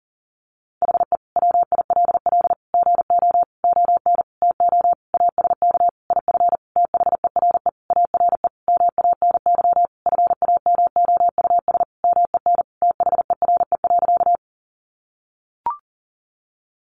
Morse code, heard as a keyed tone, about 40 wpm, 700 Hz. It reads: HE WILL GO ON TO ASK IF THERE ARE MANY FAMOUS MEN THERE.
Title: Sentences from Top 1500 Words 40wpm